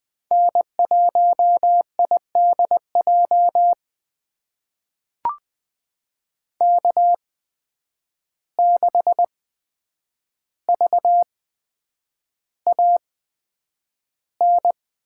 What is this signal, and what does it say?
Morse code, heard as a keyed tone, about 20 wpm, spaced as for 9 wpm, 700 Hz. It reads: N1IDJ K6VAN
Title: USA Call Signs - 5 Characters - 8x Character Spacing - 20wpm